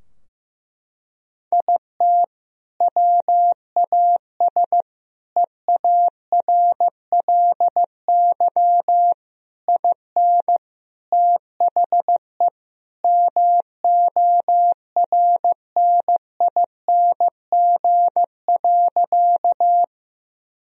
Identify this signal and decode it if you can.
Morse code, heard as a keyed tone, about 15 wpm, 700 Hz. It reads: IT WAS EARLY IN THE MORNING.
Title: Sentences from Top 1000 Words - Review Part 2 - 15wpm